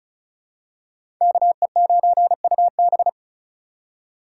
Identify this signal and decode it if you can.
Morse code, heard as a keyed tone, about 36 wpm, 700 Hz. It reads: KE9UB
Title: USA Call Signs - 5 Characters - 8x Character Spacing - 35wpm